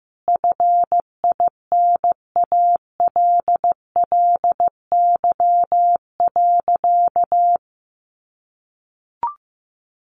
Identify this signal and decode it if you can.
Morse code, heard as a keyed tone, about 14 wpm, 700 Hz. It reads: FINALLY.
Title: Sentences from Top 1000 Words 15wpm